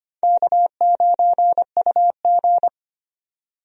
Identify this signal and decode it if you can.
Morse code, heard as a keyed tone, about 25 wpm, 700 Hz. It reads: K9UG
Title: USA Call Signs - 4 Characters - 25wpm